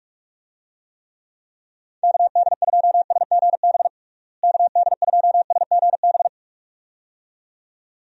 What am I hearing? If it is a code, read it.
Morse code, heard as a keyed tone, about 44 wpm, 700 Hz. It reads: KD2SGB KD2SGB
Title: USA Call Signs - 6 Characters - Repeated 2x - 45wpm